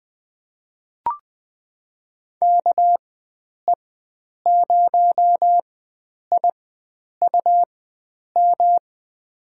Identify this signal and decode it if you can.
Morse code, heard as a keyed tone, about 20 wpm, spaced as for 9 wpm, 700 Hz. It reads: KE0IUM
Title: USA Call Signs - 6 Characters - 4x Character Spacing - 20wpm